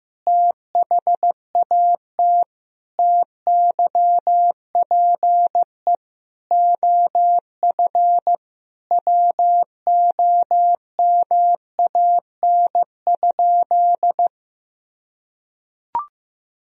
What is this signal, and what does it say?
Morse code, heard as a keyed tone, about 15 wpm, 700 Hz. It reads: THAT TYPE OF WOMAN?
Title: Sentences from Top 1000 Words - Review Part 3 - 15wpm